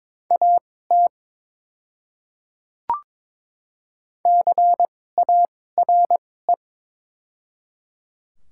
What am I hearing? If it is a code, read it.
Morse code, heard as a keyed tone, about 22 wpm, spaced as for 16 wpm, 700 Hz. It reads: AT CARE
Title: 2x Character Spacing - Top 300 Words - 22wpm